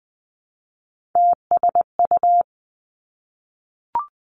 Morse code, heard as a keyed tone, about 19 wpm, 700 Hz. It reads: TSU
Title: Three Challenging Letter-Number Combinations 20wpm